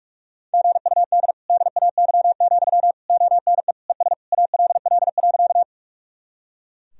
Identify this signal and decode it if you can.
Morse code, heard as a keyed tone, about 42 wpm, 700 Hz. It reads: GUD DAY, ONE ES ALL.
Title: Sentences from Top 200 Words and Abbreviations 45wpm